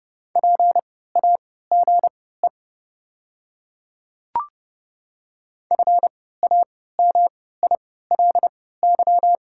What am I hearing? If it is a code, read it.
Morse code, heard as a keyed tone, about 32 wpm, spaced as for 17 wpm, 700 Hz. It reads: PAGE FAMILY
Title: 3x Character Spacing - Top 500 Words - 30wpm